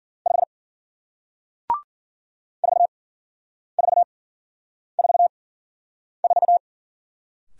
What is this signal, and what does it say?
Morse code, heard as a keyed tone, about 54 wpm, spaced as for 9 wpm, 700 Hz. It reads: 5 4444